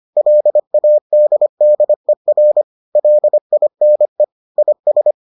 Morse code, heard as a keyed tone, about 25 wpm, 600 Hz. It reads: LADDER LINE IS